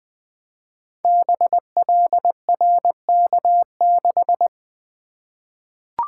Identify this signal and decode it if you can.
Morse code, heard as a keyed tone, about 20 wpm, 700 Hz. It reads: BLRK6
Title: Code Groups - 5 Letter:Numbers - 20wpm